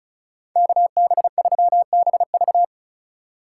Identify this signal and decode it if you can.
Morse code, heard as a keyed tone, about 36 wpm, 700 Hz. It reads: KB3BV
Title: USA Call Signs - 5 Characters - 4x Character Spacing - 35wpm